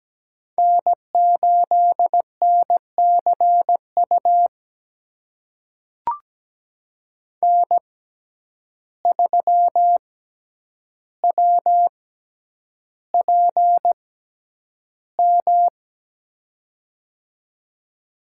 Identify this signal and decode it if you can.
Morse code, heard as a keyed tone, about 17 wpm, spaced as for 9 wpm, 700 Hz. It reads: N8NCU N3WPM
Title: USA Call Signs - 5 Characters - 6x Character Spacing - 17wpm